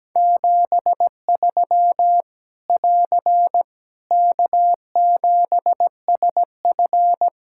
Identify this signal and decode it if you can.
Morse code, heard as a keyed tone, about 17 wpm, 700 Hz. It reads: 73 <AR> K7SF
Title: Practice QSO - Exchange 4 - Ending - 17wpm